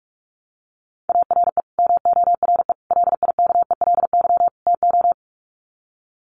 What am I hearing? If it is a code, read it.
Morse code, heard as a keyed tone, about 44 wpm, 700 Hz. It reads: ARE MORE LIKELY TO
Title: Sets of 4 Words 45wpm